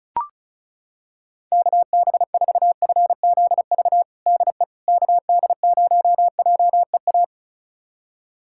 Morse code, heard as a keyed tone, about 36 wpm, 700 Hz. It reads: KB4FZV DE KD0JEA
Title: USA Callsign DE Callsign - 6 Characters - 35wpm